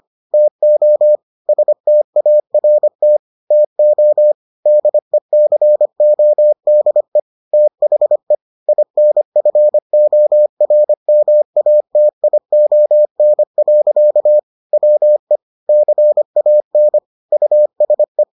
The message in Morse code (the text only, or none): TO START TO DECODE THE INFORMATION. WE CAN USE